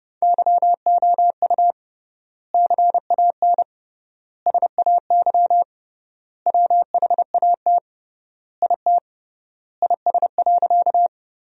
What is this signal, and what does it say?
Morse code, heard as a keyed tone, about 30 wpm, 700 Hz. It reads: YOU CAN SAY WHAT IT IS.
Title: 3x Word-Spacing - Sentences from Top 200 Words - 30wpm